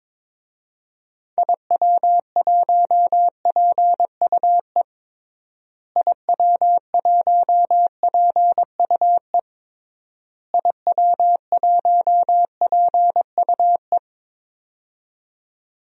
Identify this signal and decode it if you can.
Morse code, heard as a keyed tone, about 22 wpm, 700 Hz. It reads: IW1PUE IW1PUE IW1PUE
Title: DXCC-100 Call Signs - 6 Characters - Repeated 3x - 22wpm